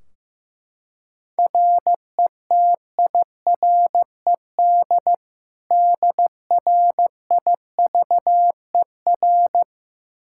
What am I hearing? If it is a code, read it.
Morse code, heard as a keyed tone, about 15 wpm, 700 Hz. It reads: RETIRED DRIVER
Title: K1USN SST QSO Practice 15wpm